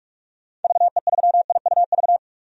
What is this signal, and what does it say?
Morse code, heard as a keyed tone, about 42 wpm, 700 Hz. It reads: VE3IUV
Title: DXCC-100 Call Signs - 6 Characters - 45wpm